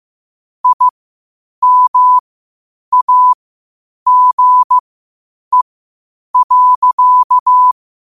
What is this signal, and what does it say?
Morse code, heard as a keyed tone, about 15 wpm, spaced as for 8 wpm, 1000 Hz. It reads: IMAGE.
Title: 8 WPM morse code quotes for Fri, 01 Mar 2024